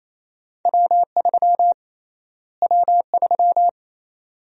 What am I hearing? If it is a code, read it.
Morse code, heard as a keyed tone, about 28 wpm, 700 Hz. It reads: W3 W3
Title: USA Call Sign Prefixes - Repeated 2x - 28wpm